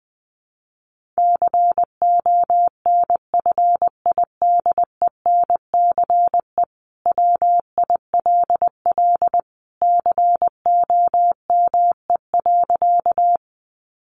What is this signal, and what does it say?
Morse code, heard as a keyed tone, about 20 wpm, 700 Hz. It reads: CONFIDENCE WILL COME.
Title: Sentences from Top 1500 Words 20wpm